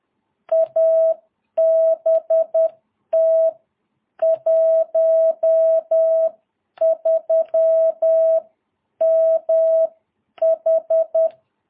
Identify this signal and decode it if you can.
Morse code, heard as a keyed tone, about 10 wpm, spaced as for 9 wpm, 660 Hz. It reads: ABT 13MH
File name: jh8cbh / 2022cw